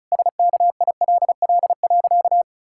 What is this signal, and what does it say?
Morse code, heard as a keyed tone, about 34 wpm, 700 Hz. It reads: SKILL.
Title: Pangram Challenge v2 - 35wpm